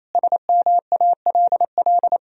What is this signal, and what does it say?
Morse code, heard as a keyed tone, about 28 wpm, 700 Hz. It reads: SMALL
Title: Sentences from Top 1000 Words - Review Part 3 - 28wpm